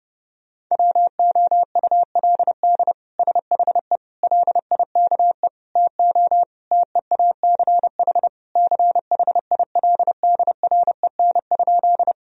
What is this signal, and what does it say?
Morse code, heard as a keyed tone, about 30 wpm, 700 Hz. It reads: WOULD SHE LIKE TO TEACH CHILDREN?
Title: Sentences from Top 1000 Words - Review Part 2 - 30wpm